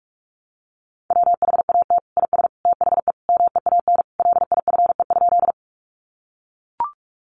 Morse code, heard as a keyed tone, about 44 wpm, 700 Hz. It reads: WHAT IS THE MEAN LIFE?
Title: Sentences from Top 900 Words 45wpm